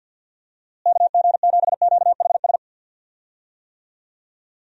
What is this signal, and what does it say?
Morse code, heard as a keyed tone, about 53 wpm, 700 Hz. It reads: KG7QHS
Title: USA Call Signs - 6 Characters - Repeated 2x - 50wpm